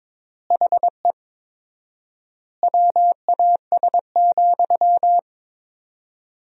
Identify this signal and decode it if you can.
Morse code, heard as a keyed tone, about 21 wpm, 700 Hz. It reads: HE WAS,
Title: 4x Word-Spacing - Sentences from Top 400 Words - 22wpm